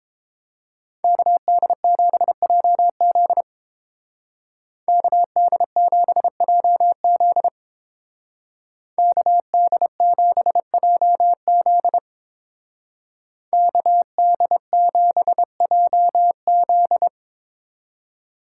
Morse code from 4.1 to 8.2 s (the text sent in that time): KD7JZ